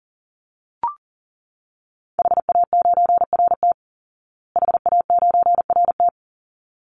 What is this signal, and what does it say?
Morse code, heard as a keyed tone, about 40 wpm, 700 Hz. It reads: HA9RT HA9RT
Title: Call Signs Repeated 3x - 40wpm